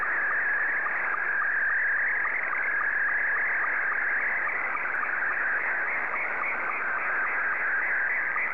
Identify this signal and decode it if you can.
SSTV (slow-scan television), Scottie 2.
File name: sstv / S2-ZL2CC